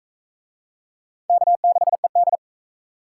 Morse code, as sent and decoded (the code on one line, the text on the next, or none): -.- -.... -..
K6ED